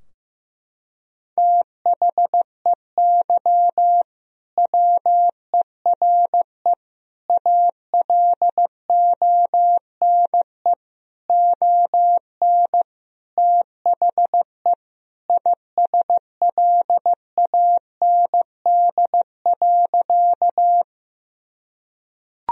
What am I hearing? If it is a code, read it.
Morse code, heard as a keyed tone, about 15 wpm, 700 Hz. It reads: THEY WERE ALONE ON THE ISLAND.
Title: Sentences from Top 1000 Words - Review Part 3 - 15wpm